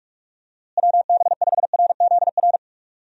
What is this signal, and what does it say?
Morse code, heard as a keyed tone, about 46 wpm, 700 Hz. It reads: WB5RZR
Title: USA Call Signs - 6 Characters - Repeated 3x - 45wpm